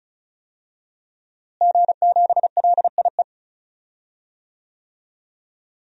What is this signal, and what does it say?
Morse code, heard as a keyed tone, about 36 wpm, 700 Hz. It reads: G7LIE
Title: DXCC-100 Call Signs - 5 Characters - Repeated 2x - 35wpm